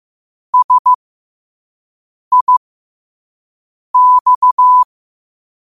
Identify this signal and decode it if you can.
Morse code, heard as a keyed tone, about 15 wpm, spaced as for 5 wpm, 1000 Hz. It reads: SIX
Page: New quotes every day in morse code at 5 Words per minute.